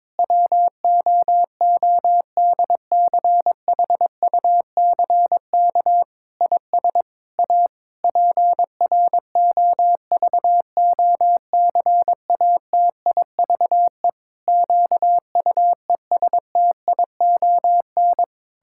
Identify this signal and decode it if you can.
Morse code, heard as a keyed tone, about 22 wpm, 700 Hz. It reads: WOODCHUCK IS A PROVOCATIVE QUESTION
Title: Pangram Challenge v2 - 22wpm